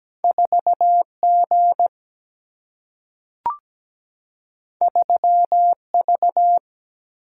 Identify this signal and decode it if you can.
Morse code, heard as a keyed tone, about 17 wpm, 700 Hz. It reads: 4G 3V
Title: Single-Letter Single-Number 17wpm